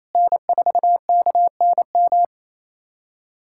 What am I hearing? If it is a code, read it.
Morse code, heard as a keyed tone, about 28 wpm, 700 Hz. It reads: N4KNM
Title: USA Callsign DE Callsign - 5 Characters - 28wpm